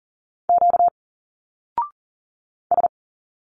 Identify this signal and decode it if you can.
Morse code, heard as a keyed tone, about 39 wpm, 700 Hz. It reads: Q S
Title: Single Letters 40wpm